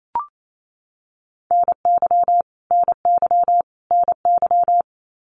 Morse code, heard as a keyed tone, about 28 wpm, 700 Hz. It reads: NY NY NY
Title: US States Repeated 3x 28wpm